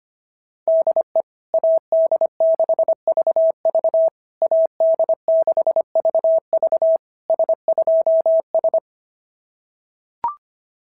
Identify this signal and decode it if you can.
Morse code, heard as a keyed tone, about 24 wpm, 650 Hz. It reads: DE AD6VV AD6VV S2S
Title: SOTA QSO Practice 25wpm